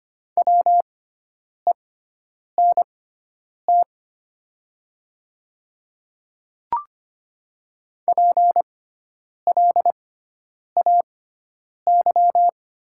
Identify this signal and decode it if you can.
Morse code, heard as a keyed tone, about 25 wpm, spaced as for 9 wpm, 700 Hz. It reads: WENT PLAY
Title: 6x Character Spacing - Top 200 Words - 25wpm